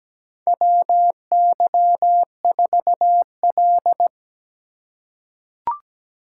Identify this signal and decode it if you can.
Morse code, heard as a keyed tone, about 17 wpm, 700 Hz. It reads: WY4L
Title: USA Call Signs - 2x Character Spacing - 17wpm